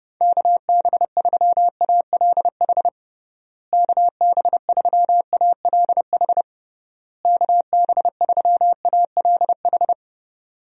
Morse code, heard as a keyed tone, about 30 wpm, 700 Hz. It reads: KB3ALH KB3ALH KB3ALH
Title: USA Call Signs - 6 Characters - Repeated 3x - 30wpm